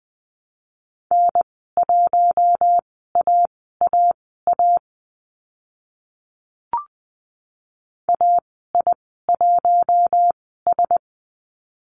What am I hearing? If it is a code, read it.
Morse code, heard as a keyed tone, about 20 wpm, spaced as for 15 wpm, 700 Hz. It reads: N1AAA AI1S
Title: 042 Lesson 8g - Number 1 - Call Signs 20wpm